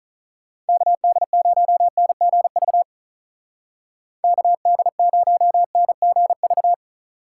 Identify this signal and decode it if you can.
Morse code, heard as a keyed tone, about 47 wpm, 700 Hz. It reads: KD0NGV KD0NGV